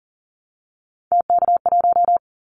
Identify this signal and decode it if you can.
Morse code, heard as a keyed tone, about 40 wpm, 700 Hz. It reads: TK1
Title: Callsign Prefixes for Top 100 DXCC - 40wpm